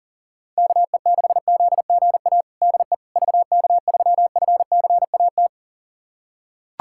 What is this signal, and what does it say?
Morse code, heard as a keyed tone, about 40 wpm, 700 Hz. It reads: KE6ZGA DE VK3FCAT
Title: USA Callsign DE Callsign 40wpm